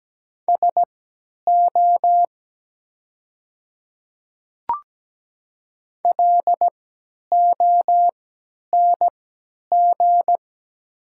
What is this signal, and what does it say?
Morse code, heard as a keyed tone, about 17 wpm, spaced as for 10 wpm, 700 Hz. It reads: SO LONG
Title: 3x Character Spacing - Top 100 Words - 17wpm